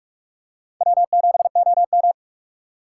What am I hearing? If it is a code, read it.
Morse code, heard as a keyed tone, about 44 wpm, 690 Hz. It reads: W7OM
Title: USA Call Signs - 4 Characters - 2x Character Spacing - 45wpm